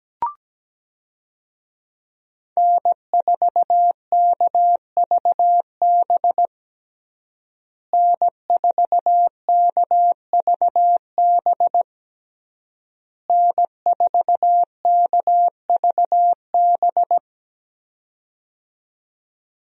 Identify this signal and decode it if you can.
Morse code, heard as a keyed tone, about 17 wpm, 700 Hz. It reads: N4KVB N4KVB N4KVB
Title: USA Call Signs - 5 Characters - Repeated 3x - 17wpm